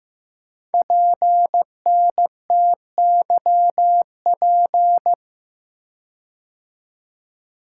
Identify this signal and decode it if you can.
Morse code, heard as a keyed tone, about 15 wpm, 700 Hz. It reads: PNTYP
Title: Code Groups - 5 Letter:Numbers - 15wpm